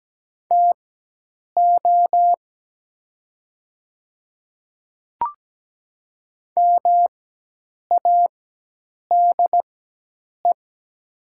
Morse code, heard as a keyed tone, about 17 wpm, spaced as for 8 wpm, 700 Hz. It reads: TO MADE
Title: 4x Character Spacing - Top 200 Words - 17wpm